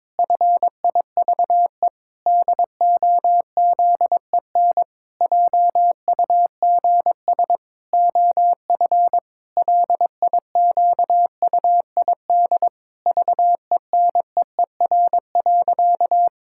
Morse code, heard as a keyed tone, about 22 wpm, 700 Hz. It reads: FIVE DOZEN JUGS OF LIQUID VENEER.
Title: Pangram Challenge v2 - 22wpm